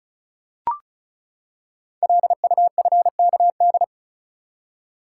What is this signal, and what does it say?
Morse code, heard as a keyed tone, about 33 wpm, 690 Hz. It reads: LUFKD
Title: Code Groups - 5 Letters - 35wpm